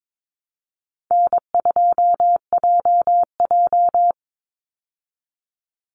Morse code, heard as a keyed tone, about 22 wpm, 700 Hz. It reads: N2JJ
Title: Easy Call Signs Repeated 3x - 22wpm